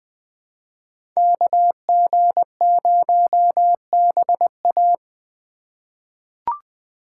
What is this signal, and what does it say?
Morse code, heard as a keyed tone, about 20 wpm, 700 Hz. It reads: KG0BA
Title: USA Call Signs - 3x Character Spacing - 20wpm